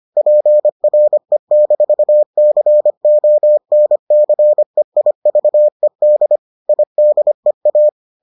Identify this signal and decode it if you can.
Morse code, heard as a keyed tone, about 25 wpm, 600 Hz. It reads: PRE-CONCEIVED IDEA